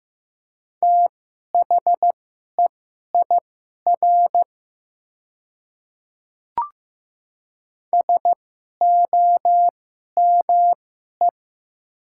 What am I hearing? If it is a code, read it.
Morse code, heard as a keyed tone, about 15 wpm, spaced as for 11 wpm, 700 Hz. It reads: THEIR SOME
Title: 2x Character Spacing - Top 200 Words - 15wpm